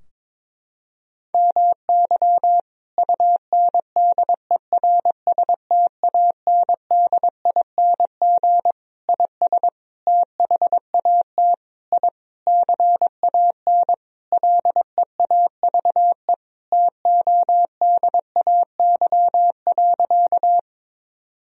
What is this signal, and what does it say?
Morse code, heard as a keyed tone, about 22 wpm, 700 Hz. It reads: MY UNDERSTANDING IS THAT I CAN LEAVE TODAY.
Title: Sentences from Top 1000 Words - Review Part 3 - 22wpm